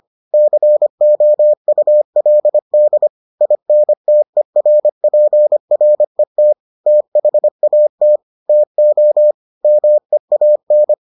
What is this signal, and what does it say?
Morse code, heard as a keyed tone, about 25 wpm, 600 Hz. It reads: COULD INTERPRET THAT TO MEAN